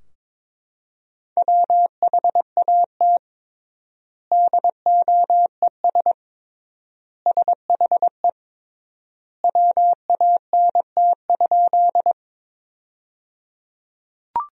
Morse code, heard as a keyed tone, about 22 wpm, 700 Hz. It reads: WHAT DOES SHE WANT?
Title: 3x Word-Spacing - Sentences from Top 200 Words - 22wpm